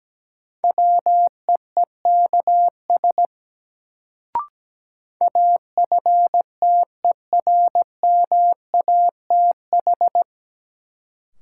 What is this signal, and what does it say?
Morse code, heard as a keyed tone, about 17 wpm, 700 Hz. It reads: WEEKS AFTERMATH
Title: Sets of 5 Words - New Word Drill - 17wpm